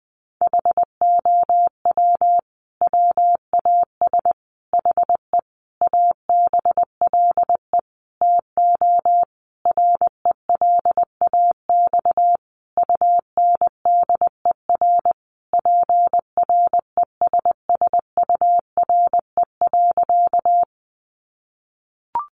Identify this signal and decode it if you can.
Morse code, heard as a keyed tone, about 19 wpm, 700 Hz. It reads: HOW WAS HE ABLE TO RELAX UNDER PRESSURE.
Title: Sentences from Top 1400 Words 20wpm